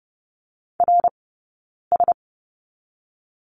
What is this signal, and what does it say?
Morse code, heard as a keyed tone, about 30 wpm, 700 Hz. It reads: R S
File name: Lesson 17d - Letter u - Two Characters 30wpm